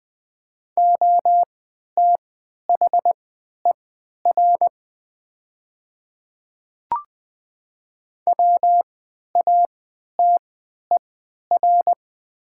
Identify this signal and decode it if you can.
Morse code, heard as a keyed tone, about 20 wpm, spaced as for 11 wpm, 700 Hz. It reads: OTHER WATER
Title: 3x Character Spacing - Top 100 Words - 20wpm